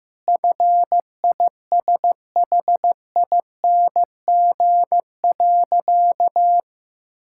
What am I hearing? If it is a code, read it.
Morse code, heard as a keyed tone, about 15 wpm, 700 Hz. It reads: FISHING.
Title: Sentences from Top 1000 Words - Review Part 6 - 15wpm